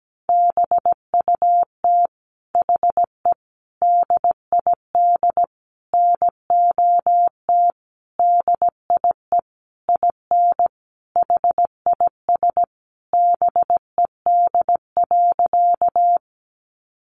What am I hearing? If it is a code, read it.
Morse code, heard as a keyed tone, about 17 wpm, 700 Hz. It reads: BUT HE DID NOT DIE IN HIS BED.
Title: Sentences from Top 1300 Words 17wpm